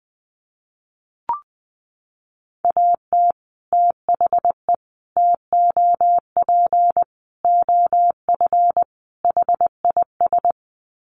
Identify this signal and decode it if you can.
Morse code, heard as a keyed tone, about 19 wpm, 700 Hz. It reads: AT THE TOP OF HIS
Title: Sets of 5 Words Encore 20wpm